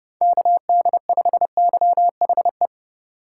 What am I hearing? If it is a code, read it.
Morse code, heard as a keyed tone, about 30 wpm, 700 Hz. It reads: KD5YHE